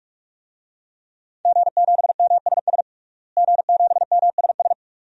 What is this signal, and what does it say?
Morse code, heard as a keyed tone, about 46 wpm, 700 Hz. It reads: G7MSS G7MSS
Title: DXCC-100 Call Signs - 5 Characters - Repeated 2x - 45wpm